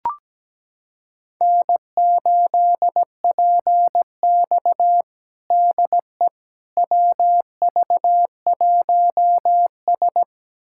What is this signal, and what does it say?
Morse code, heard as a keyed tone, about 17 wpm, 700 Hz. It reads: N8PX DE WV1S
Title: USA Callsign DE Callsign - 4 Characters - 17wpm